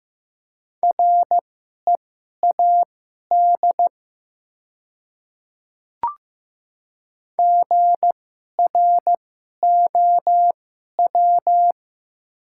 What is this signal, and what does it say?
Morse code, heard as a keyed tone, about 15 wpm, spaced as for 11 wpm, 690 Hz. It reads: READ GROW
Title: 2x Character Spacing - Top 300 Words - 15wpm